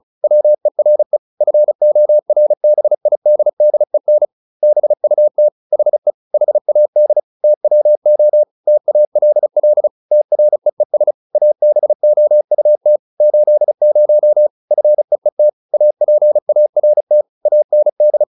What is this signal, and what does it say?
Morse code, heard as a keyed tone, about 35 wpm, 600 Hz. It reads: WERE FORBIDDEN BUT HE HAD TWO TALL TREES ABOUT 80 FEET APART AND